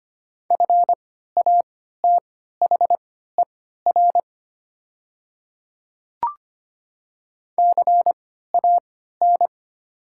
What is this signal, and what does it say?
Morse code, heard as a keyed tone, about 25 wpm, spaced as for 14 wpm, 690 Hz. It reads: FATHER CAN
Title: 3x Character Spacing - Top 300 Words - 25wpm